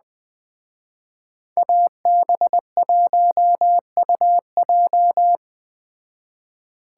Morse code, heard as a keyed tone, about 20 wpm, 700 Hz. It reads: AB1UJ
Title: Speed Racing - USA Call Signs 17wpm